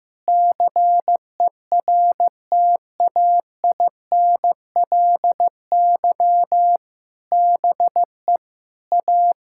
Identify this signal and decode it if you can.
Morse code, heard as a keyed tone, about 15 wpm, 700 Hz. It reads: CERTAINLY BE A
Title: Sentences from Top 1600 Words 15wpm